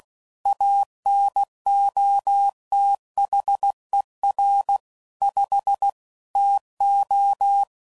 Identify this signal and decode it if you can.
Morse code, heard as a keyed tone, about 16 wpm, 800 Hz. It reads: ANOTHER 5 TO
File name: CW Code Groups @15WPM / QSOs / WA7QOX